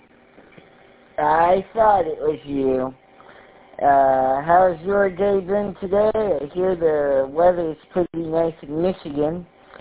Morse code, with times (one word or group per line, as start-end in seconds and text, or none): none